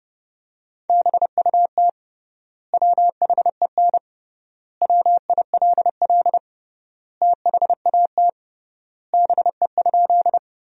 Morse code, heard as a keyed tone, about 30 wpm, 700 Hz. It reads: BUT WHEN WILL THAT BE?
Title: 3x Word-Spacing - Sentences from Top 100 Words - 30wpm